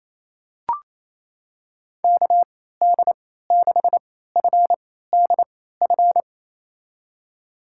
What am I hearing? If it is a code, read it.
Morse code, heard as a keyed tone, about 28 wpm, spaced as for 16 wpm, 700 Hz. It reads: KD6FDF
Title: USA Call Signs - 6 Characters - 3x Character Spacing - 28wpm